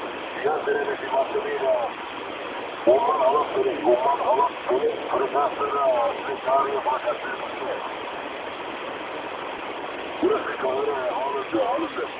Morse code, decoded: none

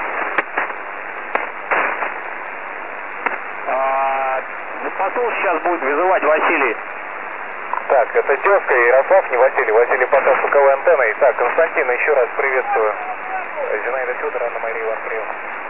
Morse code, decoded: none

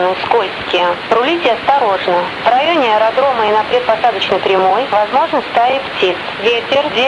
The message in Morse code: none

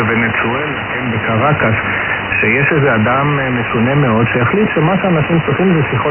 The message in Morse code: none

